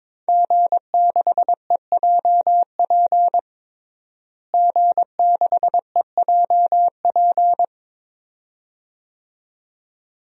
Morse code, heard as a keyed tone, about 22 wpm, 700 Hz. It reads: G6EJP G6EJP
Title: DXCC-100 Call Signs - 5 Characters - Repeated 2x - 22wpm